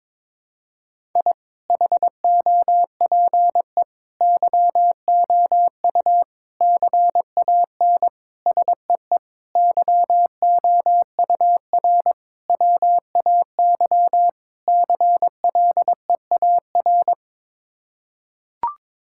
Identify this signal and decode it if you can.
Morse code, heard as a keyed tone, about 22 wpm, 700 Hz. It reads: I HOPE YOU CAN SEE YOUR WAY CLEAR
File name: practice / 22wpm / Sets of 8 Words - 22wpm